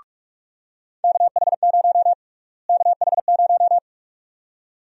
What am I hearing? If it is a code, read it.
Morse code, heard as a keyed tone, about 42 wpm, 700 Hz. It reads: KH0 KH0
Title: USA Call Sign Prefixes - Repeated 2x - 45wpm